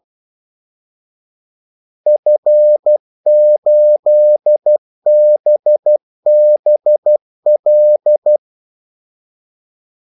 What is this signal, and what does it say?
Morse code, heard as a keyed tone, about 12 wpm, 600 Hz. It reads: F8BBL
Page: 2 ekstra ordmellemrum mellem kaldesignalerne